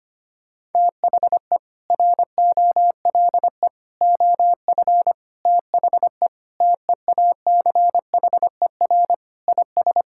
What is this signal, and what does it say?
Morse code, heard as a keyed tone, about 25 wpm, 700 Hz. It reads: THE ROLE OF THE TEACHER IS
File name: practice / 25wpm / Sentences with 4-Word Phrases - 25wpm